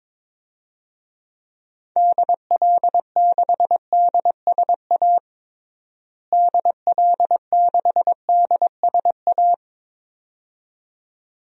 Morse code, heard as a keyed tone, about 22 wpm, 700 Hz. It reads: DL6DSA DL6DSA
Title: DXCC-100 Call Signs - 6 Characters - Repeated 2x - 22wpm